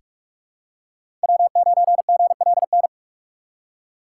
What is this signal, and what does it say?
Morse code, heard as a keyed tone, about 46 wpm, 700 Hz. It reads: W9GLN
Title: USA Call Signs - 5 Characters - 3x Character Spacing - 45wpm